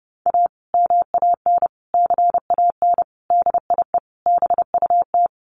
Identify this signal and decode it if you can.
Morse code, heard as a keyed tone, about 30 wpm, 700 Hz. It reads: A MAN CAN DIE BUT
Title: Sentences from Top 1300 Words 30wpm